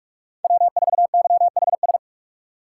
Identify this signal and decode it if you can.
Morse code, heard as a keyed tone, about 42 wpm, 700 Hz. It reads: W4YHS
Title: USA Call Signs - 5 Characters - 2x Character Spacing - 45wpm